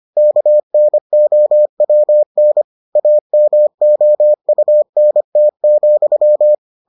Morse code, heard as a keyed tone, about 25 wpm, 600 Hz. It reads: KNOWN AMOUNT,